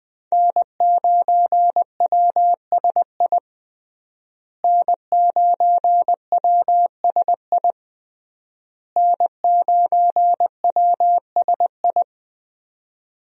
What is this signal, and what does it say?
Morse code, heard as a keyed tone, about 20 wpm, 700 Hz. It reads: N9WSI N9WSI N9WSI
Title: USA Call Signs - 5 Characters - Repeated 3x - 20wpm